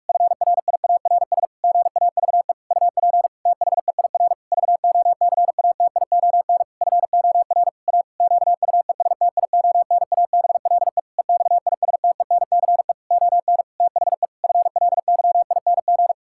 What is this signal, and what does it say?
Morse code, heard as a keyed tone, about 46 wpm, 700 Hz. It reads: FRIARS GAVE UP THEIR VOCATION FOR A QUESTIONABLE EXISTENCE ON THE FLYING
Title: Pangram Challenge v2 - 45wpm